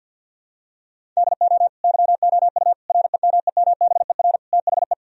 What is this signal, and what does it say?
Morse code, heard as a keyed tone, about 52 wpm, 700 Hz. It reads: DO YOU REMEMBER THE
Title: Sentences from Top 1000 Words - Review Part 1 - 50wpm